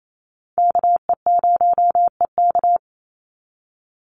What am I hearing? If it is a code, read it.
Morse code, heard as a keyed tone, about 28 wpm, 700 Hz. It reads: KE0EK
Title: Call Signs - 28wpm